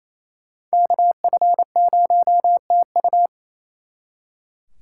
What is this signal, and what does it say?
Morse code, heard as a keyed tone, about 35 wpm, 700 Hz. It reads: KF0TU